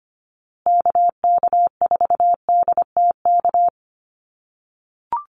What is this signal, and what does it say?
Morse code, heard as a keyed tone, about 25 wpm, 700 Hz. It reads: KK4DTK
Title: USA Call Signs Repeated 2x 25wpm